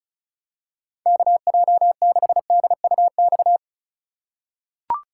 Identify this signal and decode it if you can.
Morse code, heard as a keyed tone, about 36 wpm, 700 Hz. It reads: KJ6DUX
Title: USA Call Signs - 6 Characters - 3x Character Spacing - 35wpm